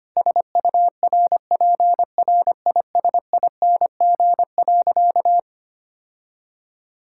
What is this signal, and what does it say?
Morse code, heard as a keyed tone, about 25 wpm, 700 Hz. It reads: SURPRISING.
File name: practice / 25wpm / Sentences with 4-Word Phrases - 25wpm